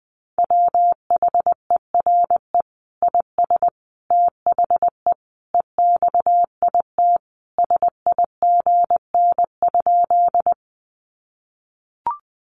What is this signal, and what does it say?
Morse code, heard as a keyed tone, about 20 wpm, 700 Hz. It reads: WHERE IS THE EXIT SIGN?
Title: Sentences from Top 1100 Words 20wpm